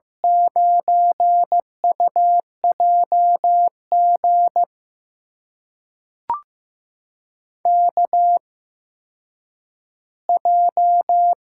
Morse code, heard as a keyed tone, about 15 wpm, spaced as for 5 wpm, 700 Hz. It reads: 9UJ GKJ
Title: USA Call Signs - 6 Characters - 8x Character Spacing - 15wpm